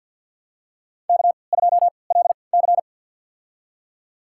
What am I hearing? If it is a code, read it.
Morse code, heard as a keyed tone, about 53 wpm, spaced as for 29 wpm, 700 Hz. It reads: K2LC
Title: USA Call Signs - 4 Characters - 3x Character Spacing - 50wpm